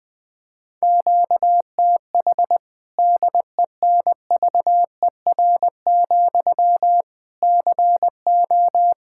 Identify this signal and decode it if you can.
Morse code, heard as a keyed tone, about 21 wpm, 700 Hz. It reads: QTH DENVER, CO
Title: K1USN SST QSO Practice 20wpm